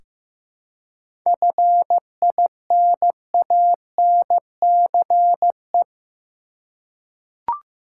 Morse code, heard as a keyed tone, about 15 wpm, 700 Hz. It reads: FINANCE
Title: Seven Letter Words - 15wpm